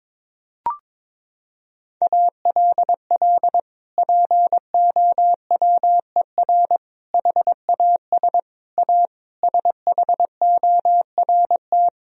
Morse code, heard as a keyed tone, about 22 wpm, 700 Hz. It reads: ALL POWER HAS A SHORT
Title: Sentences from Top 1000 Words - Review Part 2 - 22wpm